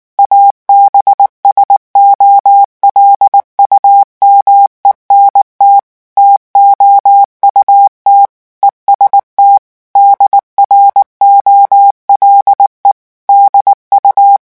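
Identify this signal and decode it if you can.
Morse code, heard as a keyed tone, about 19 wpm, 800 Hz. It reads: ABSOLUMENT TOUT EST DROLE DU
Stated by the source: French Morse revision 8 drill of random five-letter groups at 18 words per minute.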